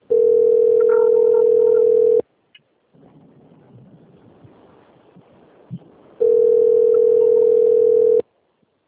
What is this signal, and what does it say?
Ringback tone (North America), 440 + 480 Hz.